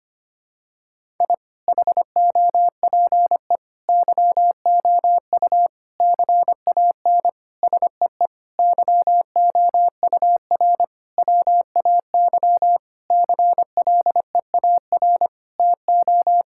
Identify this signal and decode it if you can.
Morse code, heard as a keyed tone, about 25 wpm, 700 Hz. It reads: I HOPE YOU CAN SEE YOUR WAY CLEAR TO
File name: practice / 25wpm / Sentences with 9-Word Phrases - 25wpm